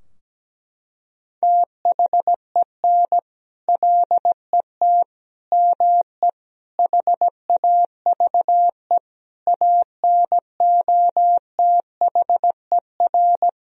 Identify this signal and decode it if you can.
Morse code, heard as a keyed tone, about 17 wpm, 700 Hz. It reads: THEN LET ME HAVE ANOTHER
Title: Sentences from Top 1000 Words - Review Part 2 - 17wpm